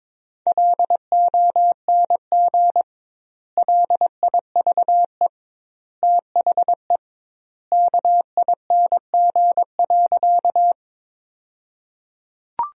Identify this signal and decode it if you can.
Morse code, heard as a keyed tone, about 22 wpm, 700 Hz. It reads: LONG LIVE THE KING.
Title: 2x Word-Spacing - Sentences from Top 400 Words - 22wpm